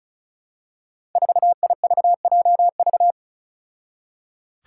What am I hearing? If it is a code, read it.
Morse code, heard as a keyed tone, about 35 wpm, 700 Hz. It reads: 4IVJV
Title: Five Challenging Letter-Number Combinations 35wpm